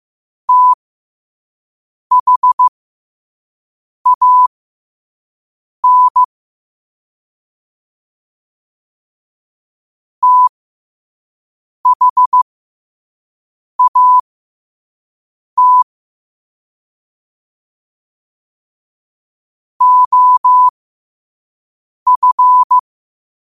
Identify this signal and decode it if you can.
Morse code, heard as a keyed tone, about 15 wpm, spaced as for 5 wpm, 1000 Hz. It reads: THAN THAT OF
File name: 5wpm